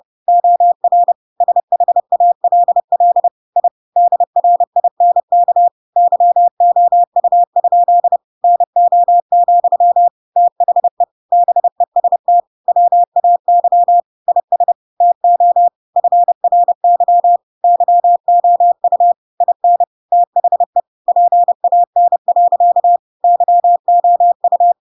Morse code, heard as a keyed tone, about 30 wpm, 700 Hz. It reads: OR SHALL I DRINK YOU? NO, THE BEST WAY IS TO FRY YOU IN THE PAN. YOU
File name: Pinocchio / Pinocchio ch5 30 30 ik1hge